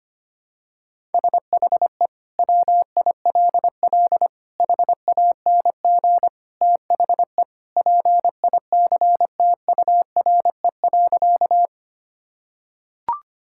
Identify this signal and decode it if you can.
Morse code, heard as a keyed tone, about 25 wpm, 700 Hz. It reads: SHE WILL HANG THE PICTURE.
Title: Sentences from Top 1600 Words 25wpm